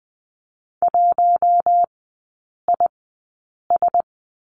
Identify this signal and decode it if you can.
Morse code, heard as a keyed tone, about 20 wpm, spaced as for 8 wpm, 700 Hz. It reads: 1IS
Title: 047 Lesson 9e - Number 4 - Three Characters 20wpm